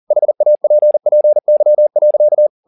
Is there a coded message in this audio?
Yes: Morse code from 0.1 to 2.5 s, heard as a keyed tone, about 40 wpm, 600 Hz. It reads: HAPPY.